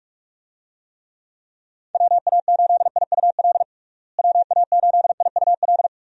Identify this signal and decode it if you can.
Morse code, heard as a keyed tone, about 46 wpm, 700 Hz. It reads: WA8IUL WA8IUL
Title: USA Call Signs - 6 Characters - Repeated 3x - 45wpm